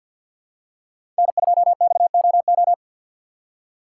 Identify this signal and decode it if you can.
Morse code, heard as a keyed tone, about 53 wpm, 700 Hz. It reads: N2XOO
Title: USA Call Signs - 5 Characters - 2x Character Spacing - 50wpm